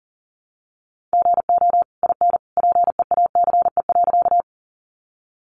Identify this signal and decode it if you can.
Morse code, heard as a keyed tone, about 40 wpm, 700 Hz. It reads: GO IN PEACE.
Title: Sentences from Top 1500 Words 40wpm